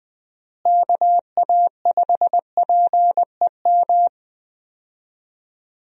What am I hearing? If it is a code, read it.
Morse code, heard as a keyed tone, about 20 wpm, 700 Hz. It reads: KA5PEM
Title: USA Call Signs - 6 Characters - Repeated 2x - 20wpm